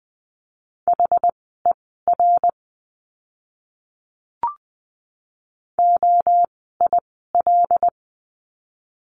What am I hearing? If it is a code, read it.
Morse code, heard as a keyed tone, about 20 wpm, spaced as for 14 wpm, 700 Hz. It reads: HER OIL
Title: 075 Lesson 13f - Letter L - Words 20wpm